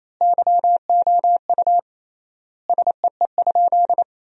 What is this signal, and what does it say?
Morse code, heard as a keyed tone, about 28 wpm, 700 Hz. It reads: YOU SEE?
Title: 3x Word-Spacing - Sentences from Top 100 Words - 28wpm